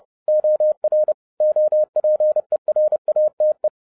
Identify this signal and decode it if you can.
Morse code, heard as a keyed tone, about 30 wpm, 610 Hz. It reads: OR OPERATE